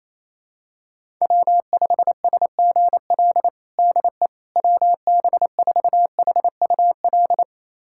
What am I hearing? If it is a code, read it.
Morse code, heard as a keyed tone, about 28 wpm, 700 Hz. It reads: W5SGL DE WB4HUL
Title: USA Callsign DE Callsign 28wpm